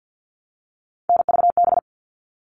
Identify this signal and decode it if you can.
Morse code, heard as a keyed tone, about 50 wpm, 700 Hz. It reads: NVB
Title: All Cut Numbers - 3-Digit 50wpm